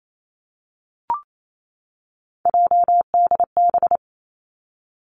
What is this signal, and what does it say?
Morse code, heard as a keyed tone, about 28 wpm, 700 Hz. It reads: JDB
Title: Three Challenging Letter-Number Combinations 28wpm